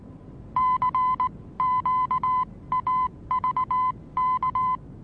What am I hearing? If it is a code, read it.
Morse code, heard as a keyed tone, about 19 wpm, spaced as for 16 wpm, 1010 Hz. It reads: CQAVK
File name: repos / Morsekurs / DJ4UF Morsen26